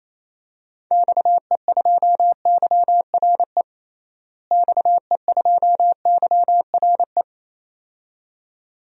Morse code, heard as a keyed tone, about 28 wpm, 700 Hz. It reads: XE2YRE XE2YRE
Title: DXCC-100 Call Signs - 6 Characters - Repeated 2x - 28wpm